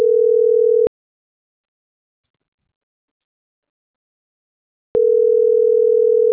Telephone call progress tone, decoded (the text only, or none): ringback tone